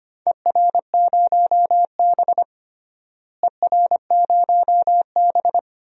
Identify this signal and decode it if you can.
Morse code, heard as a keyed tone, about 25 wpm, 700 Hz. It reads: ER0B ER0B
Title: DXCC-100 Call Signs - 4 Characters - Repeated 2x - 25wpm